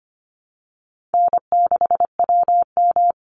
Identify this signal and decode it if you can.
Morse code, heard as a keyed tone, about 25 wpm, 700 Hz. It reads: N6WM
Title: Easy Call Signs Repeated 2x - 25wpm